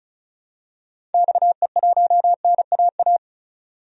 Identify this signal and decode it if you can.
Morse code, heard as a keyed tone, about 34 wpm, 700 Hz. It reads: XE1NAA
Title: DXCC-100 Call Signs - 6 Characters - 35wpm